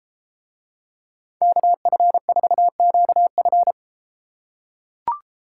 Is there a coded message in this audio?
Yes: Morse code from 1.4 to 3.7 s, heard as a keyed tone, about 32 wpm, 700 Hz. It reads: KF4QF